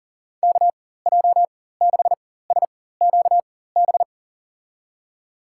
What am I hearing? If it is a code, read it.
Morse code, heard as a keyed tone, about 40 wpm, spaced as for 19 wpm, 700 Hz. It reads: KJ6SQB
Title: USA Call Signs - 6 Characters - 4x Character Spacing - 40wpm